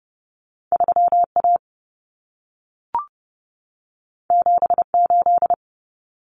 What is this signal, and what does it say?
Morse code, heard as a keyed tone, about 30 wpm, 700 Hz. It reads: 3A 78
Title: Common Cut Numbers - 2-Digit 30wpm